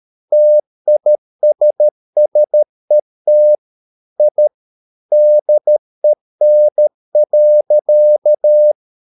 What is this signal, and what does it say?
Morse code, heard as a keyed tone, about 13 wpm, 600 Hz. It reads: TISSET I DEN.
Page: Ferie 13 wpm